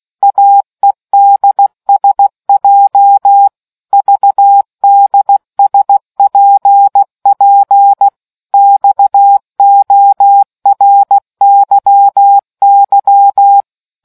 Morse code, heard as a keyed tone, about 16 wpm, 800 Hz. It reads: AEDSJ VDSPP XORYY